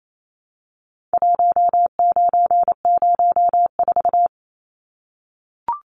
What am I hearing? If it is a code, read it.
Morse code, heard as a keyed tone, about 28 wpm, 700 Hz. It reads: 1904
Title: Numbers - 4-Digit 28wpm